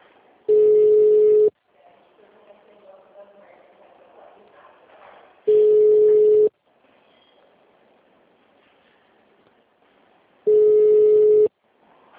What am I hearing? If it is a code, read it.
Ringback tone (Europe, 425 Hz), with speech.